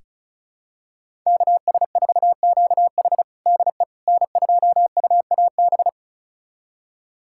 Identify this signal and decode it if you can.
Morse code, heard as a keyed tone, about 36 wpm, 700 Hz. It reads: KS4QH DE N2UAB
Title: USA Callsign DE Callsign - 5 Characters - 35wpm